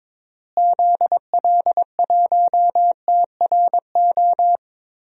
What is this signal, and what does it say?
Morse code, heard as a keyed tone, about 22 wpm, 700 Hz. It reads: ZL1TRO
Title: DXCC-100 Call Signs - 6 Characters - Repeated 2x - 22wpm